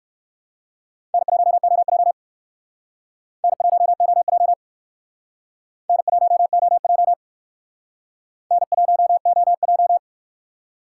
Morse code, heard as a keyed tone, about 63 wpm, 700 Hz. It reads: N1OJ N1OJ N1OJ N1OJ